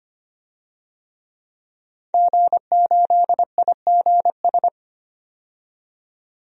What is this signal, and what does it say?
Morse code, heard as a keyed tone, about 25 wpm, 700 Hz. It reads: G8IGS
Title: DXCC-100 Call Signs - 5 Characters - 25wpm